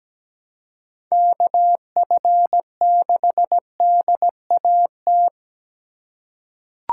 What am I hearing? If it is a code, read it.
Morse code, heard as a keyed tone, about 17 wpm, 700 Hz. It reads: KF6DAT
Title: USA Call Signs - 4x Character Spacing - 17wpm